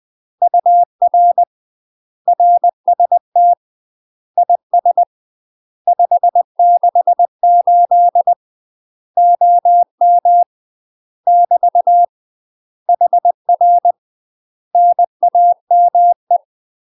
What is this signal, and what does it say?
Morse code, heard as a keyed tone, about 20 wpm, 700 Hz. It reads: UR RST IS 568 OM <BT> HR NAME
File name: UK MASTER QSOs TRAINING / 20WPM / UK MASTER QSOs 0013